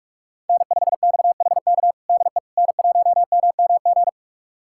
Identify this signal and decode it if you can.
Morse code, heard as a keyed tone, about 42 wpm, 700 Hz. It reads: N5XHK DE N1MMG
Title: USA Callsign DE Callsign - 5 Characters - 45wpm